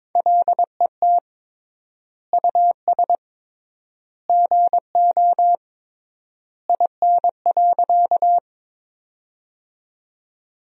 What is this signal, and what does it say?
Morse code, heard as a keyed tone, about 22 wpm, 700 Hz. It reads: LET US GO IN.
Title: 3x Word-Spacing - Sentences from Top 300 Words - 22wpm